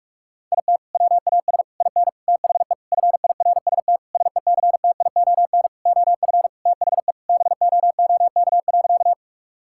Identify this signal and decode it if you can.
Morse code, heard as a keyed tone, about 41 wpm, 700 Hz. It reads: IT WAS IN THE FIRST SECTION OF THE BOOK.
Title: Sentences from Top 1000 Words - Review Part 5 - 45wpm